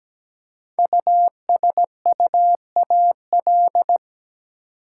Morse code, heard as a keyed tone, about 17 wpm, 700 Hz. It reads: USUAL
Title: Five Letter Words - 17wpm